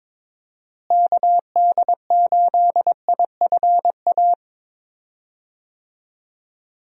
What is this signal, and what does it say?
Morse code, heard as a keyed tone, about 22 wpm, 700 Hz. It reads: KD8IFA
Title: USA Call Signs - 6 Characters - Repeated 3x - 22wpm